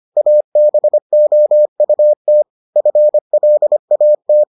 Morse code, heard as a keyed tone, about 25 wpm, 600 Hz. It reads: ABOUT FLAT